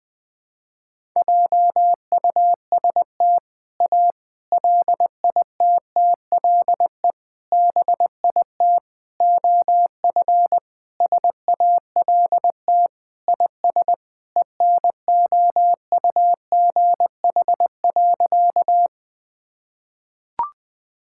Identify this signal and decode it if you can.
Morse code, heard as a keyed tone, about 20 wpm, 700 Hz. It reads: JUST A LITTLE BIT OF SALT IS ENOUGH.
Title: Sentences with 4-Word Phrases - 20wpm